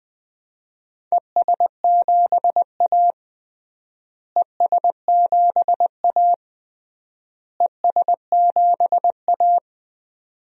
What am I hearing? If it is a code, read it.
Morse code, heard as a keyed tone, about 21 wpm, 700 Hz. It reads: ES7A ES7A ES7A
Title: DXCC-100 Call Signs - 4 Characters - Repeated 3x - 20wpm